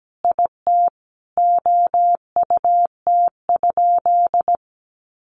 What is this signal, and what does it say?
Morse code, heard as a keyed tone, about 17 wpm, 700 Hz. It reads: IT OUT?
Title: Sentences from Top 900 Words 17wpm